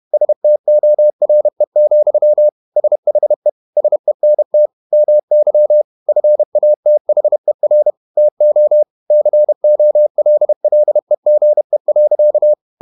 Morse code, heard as a keyed tone, about 31 wpm, 600 Hz. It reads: STORE, SHE SENT MY FATHER TO COLLEGE.
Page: Never 31wpm | CW med Gnister